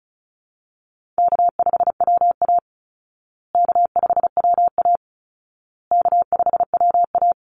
Morse code, heard as a keyed tone, about 34 wpm, 700 Hz. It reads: K5WA K5WA K5WA
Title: Call Signs Repeated 2x - 35wpm